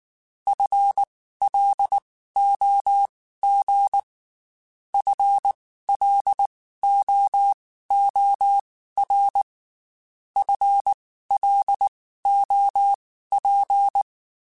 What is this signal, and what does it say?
Morse code, heard as a keyed tone, about 19 wpm, spaced as for 14 wpm, 800 Hz. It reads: FLOG FLOOR FLOP
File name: Spaced / FL Runs sp mp3s / FL Runs sp @18